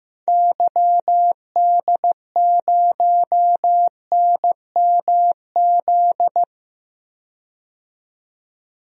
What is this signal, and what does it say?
Morse code, heard as a keyed tone, about 15 wpm, 700 Hz. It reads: YD0NMZ
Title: DXCC-100 Call Signs - 6 Characters - 15wpm